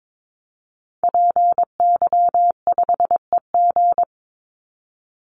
Morse code, heard as a keyed tone, about 22 wpm, 700 Hz. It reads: PY5EG
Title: Call Signs - 22wpm